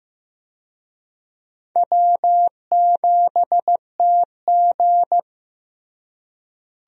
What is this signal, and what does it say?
Morse code, heard as a keyed tone, about 15 wpm, 700 Hz. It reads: W7TG
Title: USA Call Signs - 4 Characters - Repeated 2x - 15wpm